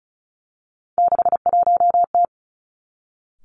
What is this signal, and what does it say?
Morse code, heard as a keyed tone, about 36 wpm, 700 Hz. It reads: 61T
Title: Three Challenging Letter-Number Combinations 35wpm